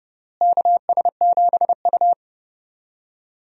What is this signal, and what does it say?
Morse code, heard as a keyed tone, about 30 wpm, 700 Hz. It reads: KS7U
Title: USA Call Signs - 4 Characters - Repeated 3x - 30wpm